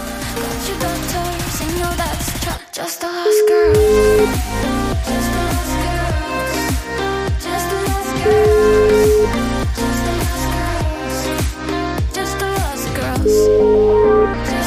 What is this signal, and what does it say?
Ringback tone (Europe, 425 Hz), with noise.